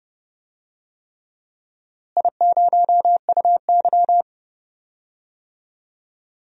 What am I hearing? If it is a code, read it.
Morse code, heard as a keyed tone, about 32 wpm, 700 Hz. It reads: I0UY
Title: DXCC-100 Call Signs - 4 Characters - Repeated 3x - 30wpm